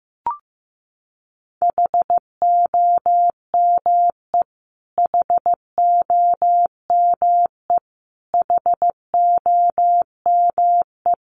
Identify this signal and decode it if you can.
Morse code, heard as a keyed tone, about 15 wpm, 700 Hz. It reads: HOME HOME HOME
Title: Top 500 Words Repeated 15wpm